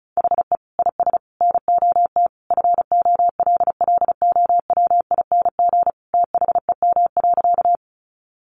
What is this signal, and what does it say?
Morse code, heard as a keyed tone, about 34 wpm, 700 Hz. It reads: HE IS NOT FOLLOWING THEM.
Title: Sentences from Top 700 Words 35wpm